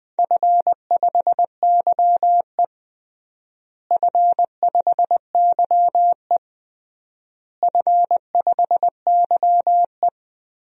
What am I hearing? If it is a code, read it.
Morse code, heard as a keyed tone, about 20 wpm, 700 Hz. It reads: F5YE F5YE F5YE
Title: DXCC-100 Call Signs - 4 Characters - Repeated 3x - 20wpm